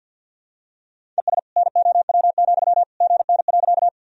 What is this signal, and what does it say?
Morse code, heard as a keyed tone, about 46 wpm, 700 Hz. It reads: ES NOW, GN.
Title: Sentences from Top 300 Words and Abbreviations 50wpm